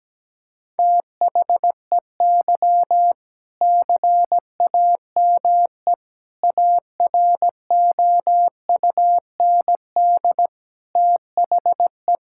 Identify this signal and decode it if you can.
Morse code, heard as a keyed tone, about 17 wpm, 700 Hz. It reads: THEY CAME AROUND THE
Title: Sentences from Top 1600 Words 17wpm